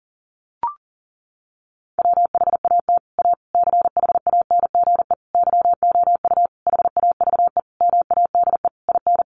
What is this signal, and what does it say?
Morse code, heard as a keyed tone, about 40 wpm, 700 Hz. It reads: WHAT A CHANGE YOU HAVE MADE IN
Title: Sentences from Top 600 Words 40wpm